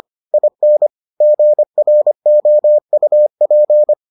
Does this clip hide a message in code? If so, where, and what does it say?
Yes: Morse code from 0.3 to 3.9 s, heard as a keyed tone, about 25 wpm, 600 Hz. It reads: IN GROUP